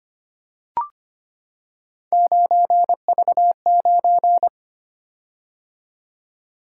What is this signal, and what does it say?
Morse code, heard as a keyed tone, about 25 wpm, 700 Hz. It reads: 9V9
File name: Three Letter-Number Combinations - 25wpm